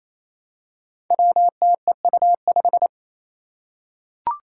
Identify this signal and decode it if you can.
Morse code, heard as a keyed tone, about 29 wpm, 700 Hz. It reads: WTEU5
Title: Five Challenging Letter-Number Combinations 28wpm